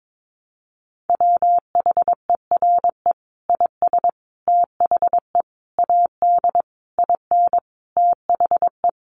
Morse code, heard as a keyed tone, about 22 wpm, 700 Hz. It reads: WHERE IS THE AD IN THE
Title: Sentences from Top 1300 Words 22wpm